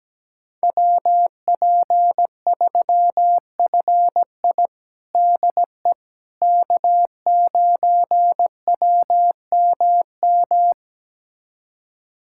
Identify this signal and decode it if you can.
Morse code, heard as a keyed tone, about 17 wpm, 700 Hz. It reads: WP3FI DE K9WMM
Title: USA Callsign DE Callsign - 5 Characters - 17wpm